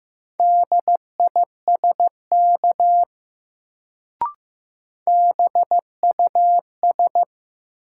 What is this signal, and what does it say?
Morse code, heard as a keyed tone, about 15 wpm, 700 Hz. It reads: DISK BUS
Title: Sentences from Top 1000 Words - Review Part 6 - New Word Drill - 15wpm